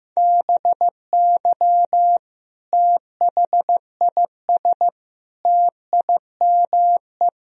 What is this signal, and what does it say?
Morse code, heard as a keyed tone, about 15 wpm, 700 Hz. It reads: BY THIS TIME
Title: Sentences from Top 1000 Words - Review Part 1 - 15wpm